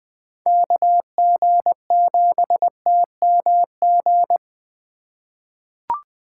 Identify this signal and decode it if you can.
Morse code, heard as a keyed tone, about 21 wpm, 700 Hz. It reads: KG7TMG
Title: USA Call Signs - 6 Characters - 2x Character Spacing - 20wpm